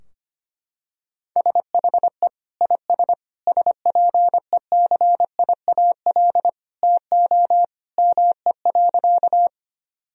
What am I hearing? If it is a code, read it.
Morse code, heard as a keyed tone, about 25 wpm, 700 Hz. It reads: SHE IS SPECIAL TO ME.
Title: Sentences from Top 1000 Words - Review Part 3 - 25wpm